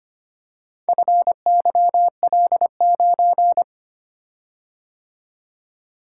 Morse code, heard as a keyed tone, about 25 wpm, 700 Hz. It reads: FYL9
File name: practice / 25wpm / Four Letter-Number Combinations - 25wpm